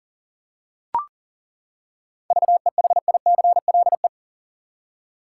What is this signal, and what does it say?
Morse code, heard as a keyed tone, about 40 wpm, 700 Hz. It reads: VEHICLE
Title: Seven Letter Words - 40wpm